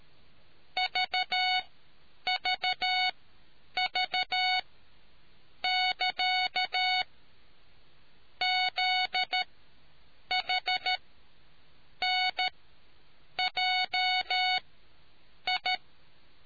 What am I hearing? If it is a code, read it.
Morse code, heard as a keyed tone, about 13 wpm, spaced as for 7 wpm, 740 Hz. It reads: VVV<KA> ZHNJI